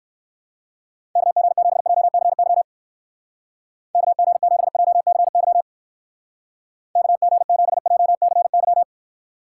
Morse code, heard as a keyed tone, about 56 wpm, 700 Hz. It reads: KG7JCY KG7JCY KG7JCY
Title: Speed Racing - USA Call Signs 45wpm